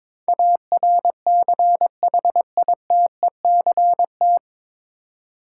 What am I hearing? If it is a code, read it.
Morse code, heard as a keyed tone, about 22 wpm, 700 Hz. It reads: ARCHITECT
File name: 9 Letter Words - First 3 Letters Given - 22wpm